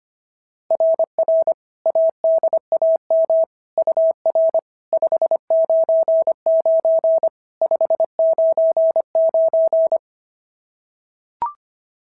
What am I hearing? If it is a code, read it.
Morse code, heard as a keyed tone, about 23 wpm, 650 Hz. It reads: RR ADAM UR 599 599
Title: SOTA QSO Practice 25wpm